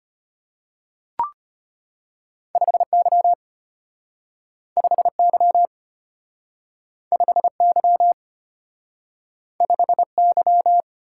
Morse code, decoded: 5Y 5Y 5Y 5Y

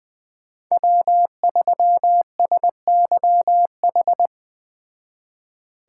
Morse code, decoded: W3SYH

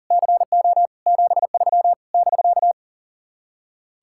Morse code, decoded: CO 73 <BK>

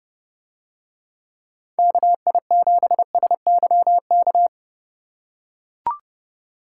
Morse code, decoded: KI7SYK

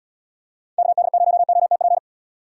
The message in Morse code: KD0OTP

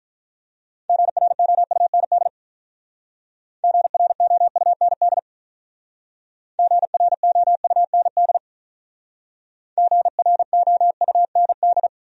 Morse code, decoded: GROUND GROUND GROUND GROUND

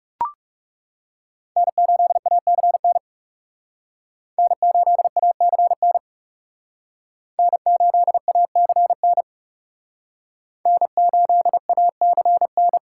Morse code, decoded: N8ACN N8ACN N8ACN N8ACN

